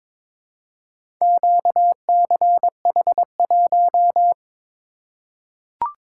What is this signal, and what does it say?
Morse code, heard as a keyed tone, about 22 wpm, 700 Hz. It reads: QCH1
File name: practice / 22wpm / Four Letter-Number Combinations - 22wpm